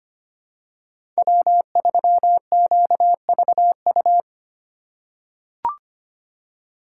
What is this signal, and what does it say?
Morse code, heard as a keyed tone, about 25 wpm, 700 Hz. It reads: W3QVU
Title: USA Call Signs - 5 Characters - 2x Character Spacing - 25wpm